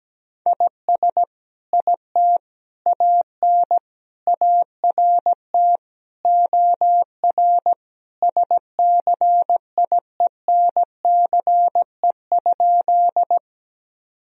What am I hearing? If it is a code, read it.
Morse code, heard as a keyed tone, about 17 wpm, 700 Hz. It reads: IS IT AN ART OR SCIENCE?
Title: Sentences from Top 1000 Words - Review Part 3 - 17wpm